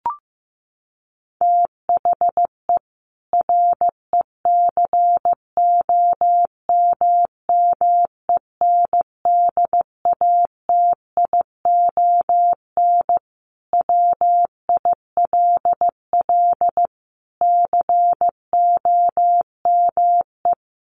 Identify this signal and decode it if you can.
Morse code, heard as a keyed tone, about 15 wpm, 700 Hz. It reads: THE RECOMMENDATION WILL COME
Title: Sentences from Top 1400 Words 15wpm